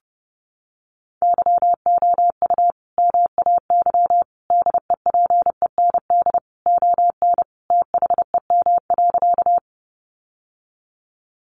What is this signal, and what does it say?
Morse code, heard as a keyed tone, about 30 wpm, 700 Hz. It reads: YOU MAY DEPEND ON THEM.
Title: Sentences from Top 1300 Words 30wpm